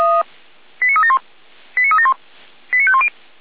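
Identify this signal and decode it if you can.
Five-tone selective call (ZVEI), heard as quick runs of five tones, tones of about 70 ms. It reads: 89371 89371 89510
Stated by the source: ZVEI five-tone selective call 89130 received via SDR.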